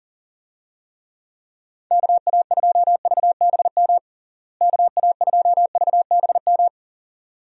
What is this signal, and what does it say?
Morse code, heard as a keyed tone, about 40 wpm, 700 Hz. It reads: KA2VBM KA2VBM
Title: USA Call Signs - 6 Characters - Repeated 2x - 40wpm